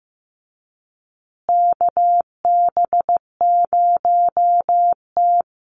Morse code, heard as a keyed tone, about 15 wpm, 700 Hz. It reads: KB0T
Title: USA Call Signs Repeated 3x 15wpm